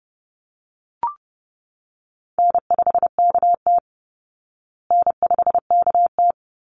Morse code, heard as a keyed tone, about 30 wpm, 700 Hz. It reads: N5KT N5KT
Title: Easy Call Signs Repeated 2x - 30wpm